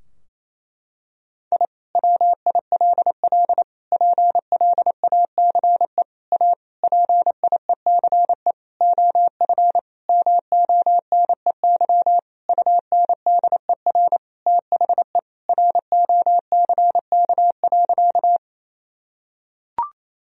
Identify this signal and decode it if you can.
Morse code, heard as a keyed tone, about 28 wpm, 700 Hz. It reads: I WILL PLACE A PIECE OF MONEY UNDER THE ROCK.
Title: Sentences from Top 1000 Words - Review Part 2 - 28wpm